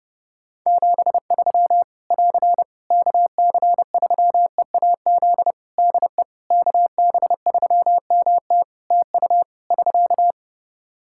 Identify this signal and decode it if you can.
Morse code, heard as a keyed tone, about 30 wpm, 700 Hz. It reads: 73 <AR> KC3EAZ DE KB3MT TU <SK>
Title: Practice QSO - Exchange 4 - Ending - 30wpm